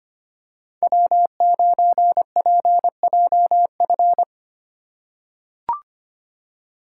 Morse code, heard as a keyed tone, about 25 wpm, 700 Hz. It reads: W9PJF
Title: USA Call Signs - 5 Characters - 2x Character Spacing - 25wpm